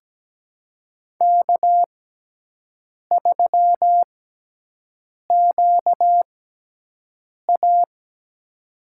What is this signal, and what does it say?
Morse code, heard as a keyed tone, about 17 wpm, spaced as for 6 wpm, 700 Hz. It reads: K3QA
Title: USA Call Signs - 6x Character Spacing - 17wpm